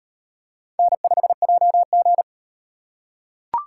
Morse code, heard as a keyed tone, about 38 wpm, 700 Hz. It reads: N5JG